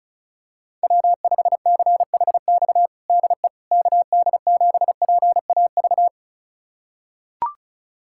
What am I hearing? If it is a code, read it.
Morse code, heard as a keyed tone, about 36 wpm, 700 Hz. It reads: W5CHX DE KD7PAV
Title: USA Callsign DE Callsign 35wpm